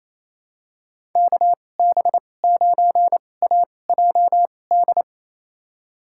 Morse code, heard as a keyed tone, about 28 wpm, spaced as for 20 wpm, 700 Hz. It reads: KB9AJD
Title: USA Call Signs - 2x Character Spacing - 28wpm